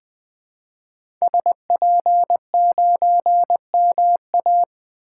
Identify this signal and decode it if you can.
Morse code, heard as a keyed tone, about 21 wpm, 700 Hz. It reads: SP9MA
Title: 2023 SOTA Activator Call Signs - 20wpm